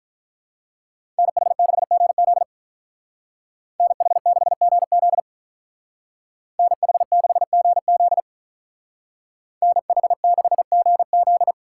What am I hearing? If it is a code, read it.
Morse code, heard as a keyed tone, about 43 wpm, 700 Hz. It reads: NH6GZ NH6GZ NH6GZ NH6GZ
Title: Speed Racing - USA 2x2 Call Signs 35wpm